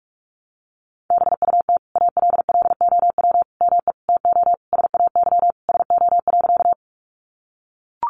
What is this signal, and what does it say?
Morse code, heard as a keyed tone, about 44 wpm, 700 Hz. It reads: BUT ALLOW ME TO SAY SO.
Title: Sentences from Top 700 Words 45wpm